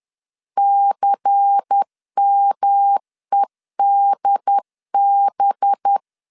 Morse code, heard as a keyed tone, about 10 wpm, 800 Hz. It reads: CMEDB